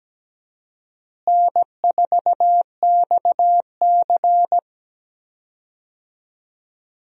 Morse code, heard as a keyed tone, about 17 wpm, 700 Hz. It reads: N4XC
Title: USA Call Signs - 4 Characters - Repeated 3x - 17wpm